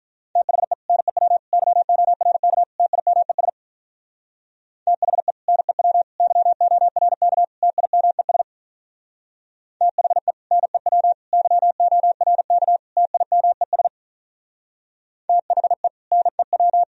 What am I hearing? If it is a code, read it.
Morse code, heard as a keyed tone, about 44 wpm, 700 Hz. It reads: THE NEW YORK TIMES THE NEW YORK TIMES THE NEW YORK TIMES THE NEW